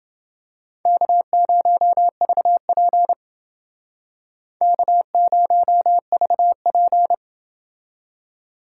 Morse code, heard as a keyed tone, about 28 wpm, 700 Hz. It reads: K0VP K0VP